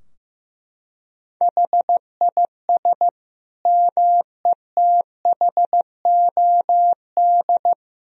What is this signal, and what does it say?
Morse code, heard as a keyed tone, about 16 wpm, 700 Hz. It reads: HIS METHOD
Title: Sentences from Top 1000 Words - Review Part 3 - 15wpm